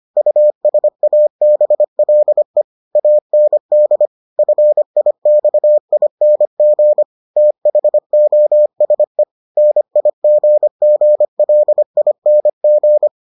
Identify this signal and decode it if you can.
Morse code, heard as a keyed tone, about 25 wpm, 600 Hz. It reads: USABLE AND FIXING THOSE NIGGLING